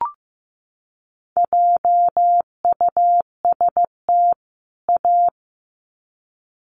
Morse code, heard as a keyed tone, about 15 wpm, 700 Hz. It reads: JUST A
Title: Sets of 2 Words 15wpm